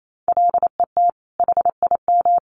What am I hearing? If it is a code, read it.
Morse code, heard as a keyed tone, about 28 wpm, 700 Hz. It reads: LET HIM
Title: Sentences from Top 1400 Words 28wpm